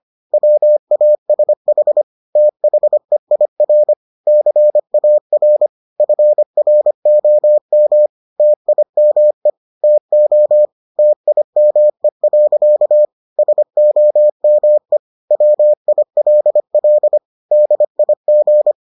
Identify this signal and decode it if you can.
Morse code, heard as a keyed tone, about 25 wpm, 600 Hz. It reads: WASH THEIR CAR FROM TIME TO TIME. SOME WILL DIG